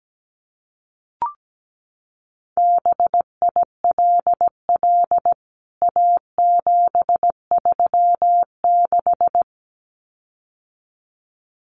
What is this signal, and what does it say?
Morse code, heard as a keyed tone, about 17 wpm, 700 Hz. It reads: BILL A736
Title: CWT Exchange 17wpm